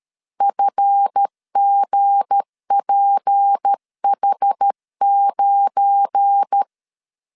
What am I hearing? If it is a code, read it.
Morse code, heard as a keyed tone, about 13 wpm, 800 Hz. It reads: FGPH9
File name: LEZIONI / 9 28 12wpm 2351